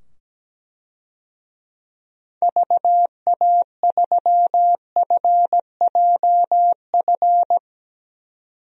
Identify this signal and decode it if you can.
Morse code, heard as a keyed tone, about 17 wpm, 700 Hz. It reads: VA3FJF
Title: DXCC-100 Call Signs - 6 Characters - 17wpm